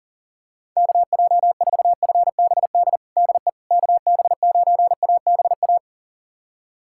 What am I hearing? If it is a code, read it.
Morse code, heard as a keyed tone, about 40 wpm, 700 Hz. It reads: KJ4FBD DE KB9ABA
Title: USA Callsign DE Callsign - 6 Characters - 40wpm